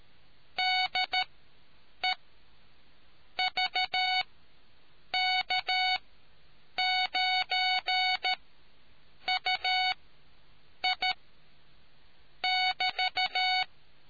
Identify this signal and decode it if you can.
Morse code, heard as a keyed tone, about 13 wpm, spaced as for 7 wpm, 740 Hz. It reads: DE VK9UI <BT>